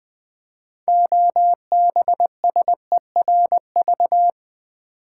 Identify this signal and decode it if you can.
Morse code, heard as a keyed tone, about 20 wpm, 700 Hz. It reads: OBSERV
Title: Build A Word - 3 Letters At A Time - 8-Letter Words - 20wpm